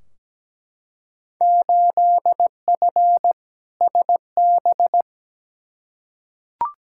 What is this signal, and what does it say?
Morse code, heard as a keyed tone, about 17 wpm, 700 Hz. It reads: 8F SB
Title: Field Day Exchanges Encore - 17wpm